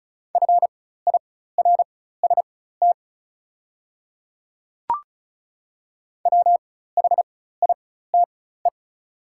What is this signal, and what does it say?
Morse code, heard as a keyed tone, about 36 wpm, spaced as for 16 wpm, 700 Hz. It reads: FIRST WHITE
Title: 4x Character Spacing - Top 300 Words - 35wpm